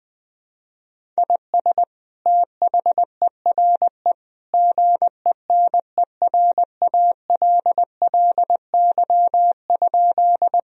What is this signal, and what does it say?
Morse code, heard as a keyed tone, about 20 wpm, 700 Hz. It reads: IS THERE GENERALLY?
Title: Sentences from Top 1000 Words - Review Part 4 - 20wpm